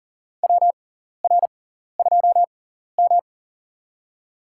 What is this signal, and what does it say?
Morse code, heard as a keyed tone, about 40 wpm, spaced as for 14 wpm, 700 Hz. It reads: WR2M
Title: USA Call Signs - 4 Characters - 6x Character Spacing - 40wpm